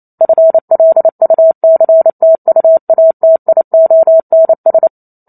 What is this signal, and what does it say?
Morse code, heard as a keyed tone, about 28 wpm, 650 Hz. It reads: FLUCTUATIONS